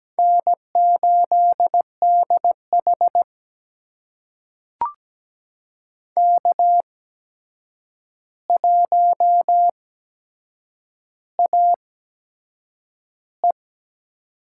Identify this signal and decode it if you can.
Morse code, heard as a keyed tone, about 17 wpm, spaced as for 7 wpm, 700 Hz. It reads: N8DH K1AE
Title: USA Call Signs - 4 Characters - 8x Character Spacing - 17wpm